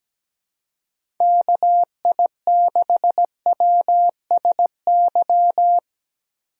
Morse code, heard as a keyed tone, about 17 wpm, 700 Hz. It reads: KI6WSY
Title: USA Call Signs - 6 Characters - 17wpm